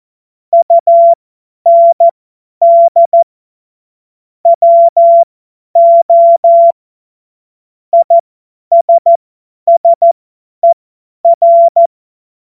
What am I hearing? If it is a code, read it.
Morse code, heard as a keyed tone, about 14 wpm, spaced as for 10 wpm, 680 Hz. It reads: UND WO ISSER